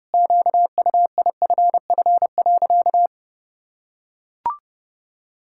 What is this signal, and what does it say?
Morse code, heard as a keyed tone, about 30 wpm, 700 Hz. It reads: QUIFF.
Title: Pangram Challenge v2 - 30wpm